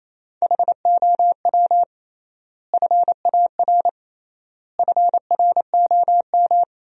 Morse code, heard as a keyed tone, about 28 wpm, 700 Hz. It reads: HOW FAR FROM
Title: 3x Word-Spacing - Sentences from Top 400 Words - 28wpm